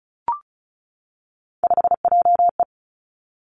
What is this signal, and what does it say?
Morse code, heard as a keyed tone, about 36 wpm, 700 Hz. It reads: 5JE
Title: Three Challenging Letter-Number Combinations 35wpm